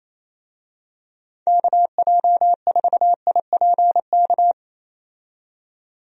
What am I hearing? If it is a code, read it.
Morse code, heard as a keyed tone, about 27 wpm, 700 Hz. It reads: KJ4IPK
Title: USA Call Signs - 6 Characters - 28wpm